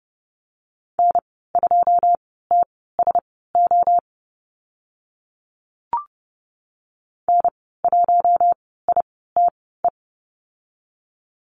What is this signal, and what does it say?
Morse code, heard as a keyed tone, about 30 wpm, spaced as for 17 wpm, 700 Hz. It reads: N2TSO N1ITE
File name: Lesson 14g - Number 2 - Call Signs 30wpm